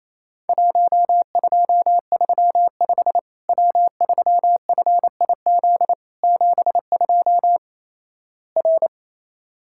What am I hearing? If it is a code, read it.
Morse code, heard as a keyed tone, about 28 wpm, 700 Hz. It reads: 1235 W3FIZ 72 R
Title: Worked All Europe DX Contest - QTC Exchange - 28wpm